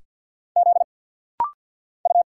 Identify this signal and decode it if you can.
Morse code, heard as a keyed tone, about 50 wpm, 700 Hz. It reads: Z U
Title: Single Letters - Rapid-Fire 50wpm